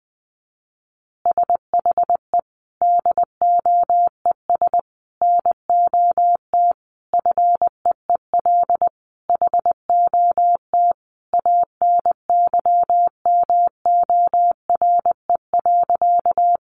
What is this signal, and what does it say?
Morse code, heard as a keyed tone, about 20 wpm, 700 Hz. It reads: SHE DOES NOT FEEL HOT ANYMORE.
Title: Additional Easy Phrases 20wpm